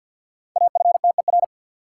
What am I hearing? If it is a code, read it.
Morse code, heard as a keyed tone, about 53 wpm, 700 Hz. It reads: AFTER
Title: Sentences from Top 200 Words - New Word Drill - 50wpm